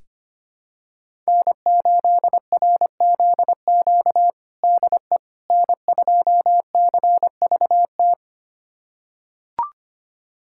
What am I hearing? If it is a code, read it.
Morse code, heard as a keyed tone, about 25 wpm, 700 Hz. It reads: N8RZQ DE N2CVT
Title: USA Callsign DE Callsign - 5 Characters - 25wpm